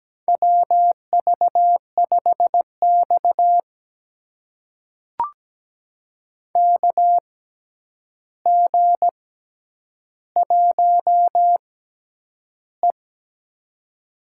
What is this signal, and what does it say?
Morse code, heard as a keyed tone, about 17 wpm, spaced as for 9 wpm, 700 Hz. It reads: WV5X KG1E
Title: USA Call Signs - 4 Characters - 6x Character Spacing - 17wpm